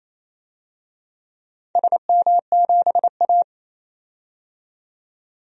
Morse code, heard as a keyed tone, about 28 wpm, 700 Hz. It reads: SM7A
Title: DXCC-100 Call Signs - 4 Characters - 28wpm